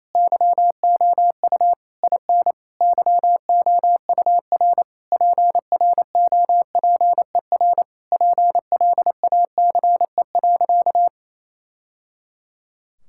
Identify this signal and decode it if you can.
Morse code, heard as a keyed tone, about 28 wpm, 700 Hz. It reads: YOU IN YOUR PROPER PLACE.
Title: Sentences from Top 1000 Words - Review Part 5 - 28wpm